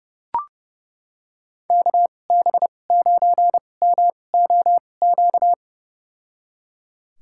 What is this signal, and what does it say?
Morse code, heard as a keyed tone, about 30 wpm, spaced as for 22 wpm, 700 Hz. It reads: KB9MOQ
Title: USA Call Signs - 6 Characters - 2x Character Spacing - 30wpm